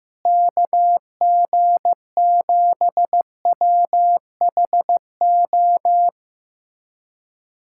Morse code, heard as a keyed tone, about 15 wpm, 700 Hz. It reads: KG7WHO
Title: USA Callsign DE Callsign 15wpm